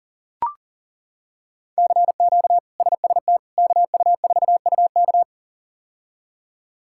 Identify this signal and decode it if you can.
Morse code, heard as a keyed tone, about 40 wpm, 700 Hz. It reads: CQ SST KU4UK
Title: K1USN SST QSO Practice - 40wpm